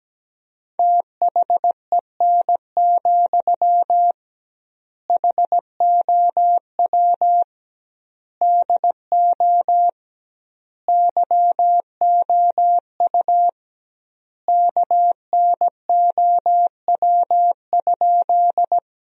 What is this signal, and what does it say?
Morse code, heard as a keyed tone, about 17 wpm, 700 Hz. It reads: THEN, HOW DO YOU KNOW?
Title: 2x Word-Spacing - Sentences from Top 100 Words - 17wpm